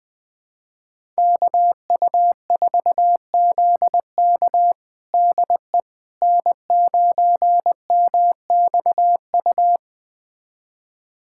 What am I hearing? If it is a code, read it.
Morse code, heard as a keyed tone, about 20 wpm, 700 Hz. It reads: KU4ZK DE N9MXU
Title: USA Callsign DE Callsign - 5 Characters - 20wpm